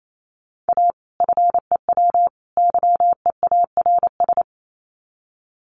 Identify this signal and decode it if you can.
Morse code, heard as a keyed tone, about 28 wpm, 700 Hz. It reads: A FEW YEARS
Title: Sets of 3 Words 28wpm